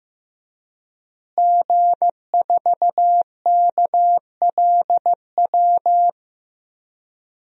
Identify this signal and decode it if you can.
Morse code, heard as a keyed tone, about 15 wpm, 700 Hz. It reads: G4KLW
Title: DXCC-100 Call Signs - 5 Characters - Repeated 3x - 15wpm